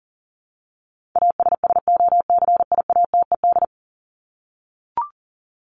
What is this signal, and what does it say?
Morse code, heard as a keyed tone, about 40 wpm, 700 Hz. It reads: ASSOCIATED
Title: ED Suffix - Top 100 Words with Base Words 40wpm